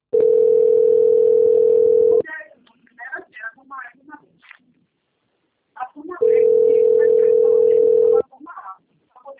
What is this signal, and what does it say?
Ringback tone (North America), 440 + 480 Hz, with speech.
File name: out- - - - - .31502